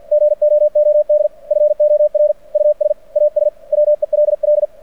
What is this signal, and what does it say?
Morse code, heard as a keyed tone, about 49 wpm, 600 Hz. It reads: GOOD FOR US AS WELL
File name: archived / CRSnet